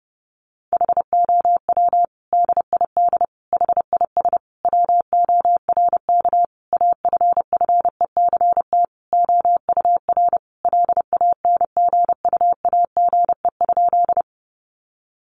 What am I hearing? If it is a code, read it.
Morse code, heard as a keyed tone, about 30 wpm, 700 Hz. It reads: HOW DID HIS WORK AFFECT OUR LANGUAGE?
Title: Sentences from Top 900 Words 30wpm